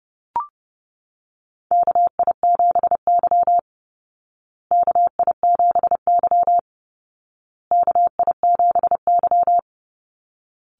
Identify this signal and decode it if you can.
Morse code, heard as a keyed tone, about 30 wpm, 700 Hz. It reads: KI7Y KI7Y KI7Y
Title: Call Signs Repeated 2x - 30wpm